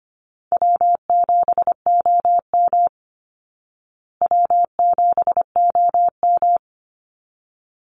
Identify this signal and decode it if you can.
Morse code, heard as a keyed tone, about 25 wpm, 700 Hz. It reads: W7OM W7OM
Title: Call Signs Repeated 3x - 25wpm